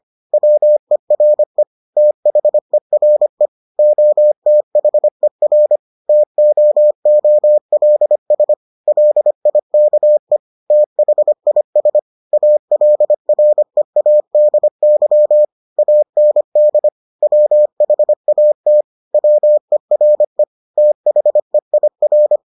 Morse code, heard as a keyed tone, about 25 wpm, 600 Hz. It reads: WERE THERE OTHER TOOLS LIKE THIS ALREADY AND WHAT WERE THEIR